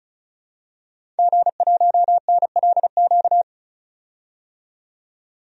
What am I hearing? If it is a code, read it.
Morse code, heard as a keyed tone, about 35 wpm, 700 Hz. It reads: G1NLQ
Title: DXCC-100 Call Signs - 5 Characters - 35wpm